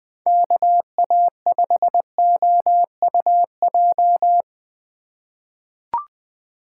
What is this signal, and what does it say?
Morse code, heard as a keyed tone, about 20 wpm, 700 Hz. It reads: KA5OUJ
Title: USA Call Signs - 6 Characters - 2x Character Spacing - 20wpm